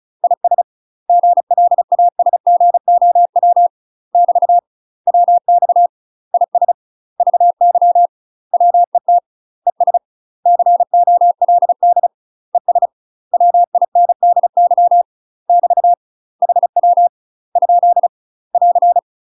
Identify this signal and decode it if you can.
Morse code, heard as a keyed tone, about 34 wpm, 700 Hz. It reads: IS GLASGOW <BT> WX IS VY WET ES COLD ES WINDY <BT> HW ? <AR>
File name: UK MASTER QSOs TRAINING / 35WPM / UK MASTER QSOs 0042